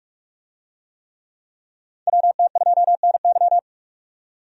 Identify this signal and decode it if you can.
Morse code, heard as a keyed tone, about 44 wpm, 700 Hz. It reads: WT2NY
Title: USA Call Signs - 5 Characters - 45wpm